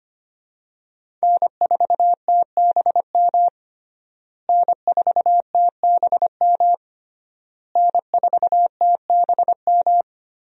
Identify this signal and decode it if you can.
Morse code, heard as a keyed tone, about 25 wpm, 700 Hz. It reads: N4TBM N4TBM N4TBM
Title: USA Call Signs - 5 Characters - Repeated 3x - 25wpm